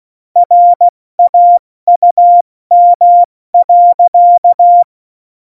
Morse code, heard as a keyed tone, about 16 wpm, spaced as for 14 wpm, 700 Hz. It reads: RAUM.
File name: Morsekurs / Witze / Witz 15. Oktober 2019, 70 F80